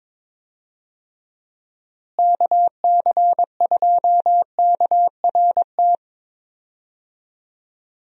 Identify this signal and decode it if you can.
Morse code, heard as a keyed tone, about 22 wpm, 700 Hz. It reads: KC2KRT
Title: USA Call Signs - 6 Characters - Repeated 2x - 22wpm